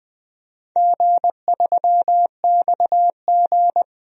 Morse code, heard as a keyed tone, about 20 wpm, 700 Hz. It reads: G3XG
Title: DXCC-100 Call Signs - 4 Characters - 20wpm